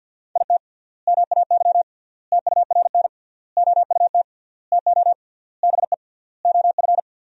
Morse code, heard as a keyed tone, about 50 wpm, 700 Hz. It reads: IT MAY TURN OUT TO BE OF
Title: 3x Word-Spacing - Sentences from Top 400 Words - 50wpm